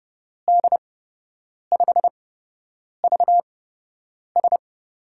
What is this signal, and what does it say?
Morse code, heard as a keyed tone, about 30 wpm, spaced as for 8 wpm, 700 Hz. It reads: D5VS
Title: USA Call Signs - 8x Character Spacing - 30wpm